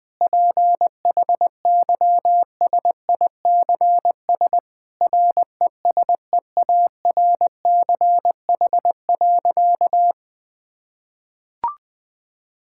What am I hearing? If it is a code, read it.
Morse code, heard as a keyed tone, about 20 wpm, 700 Hz. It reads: PHYSICS RESEARCH.
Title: Sentences from Top 1000 Words - Review Part 5 - 20wpm